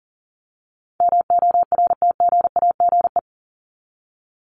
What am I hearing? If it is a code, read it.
Morse code, heard as a keyed tone, about 40 wpm, 700 Hz. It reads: MORTGAGE
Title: Top 1751-2000 Words 40wpm